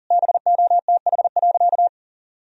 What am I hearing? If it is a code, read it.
Morse code, heard as a keyed tone, about 40 wpm, 700 Hz. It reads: BOTH.
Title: Sentences from Top 300 Words and Abbreviations 40wpm